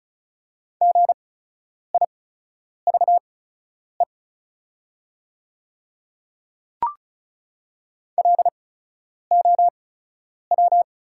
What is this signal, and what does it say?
Morse code, heard as a keyed tone, about 36 wpm, spaced as for 10 wpm, 690 Hz. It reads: GIVE LOW
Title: 8x Character Spacing - Top 200 Words - 35wpm